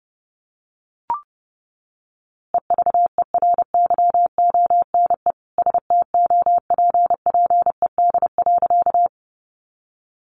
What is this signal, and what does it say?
Morse code, heard as a keyed tone, about 30 wpm, 700 Hz. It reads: EVERYONE STOPPED.
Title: A Sound Of Thunder 30wpm